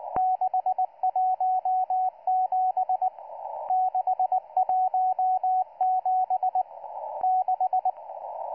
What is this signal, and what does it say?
Morse code, heard as a keyed tone, about 19 wpm, 740 Hz, with radio noise. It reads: 617 617 6